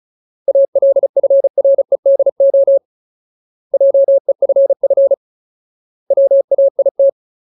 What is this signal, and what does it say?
Morse code, heard as a keyed tone, about 34 wpm, 550 Hz. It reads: ALFREDO JEFF WAIT
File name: Ultimate Word List / 35WPM Ultimate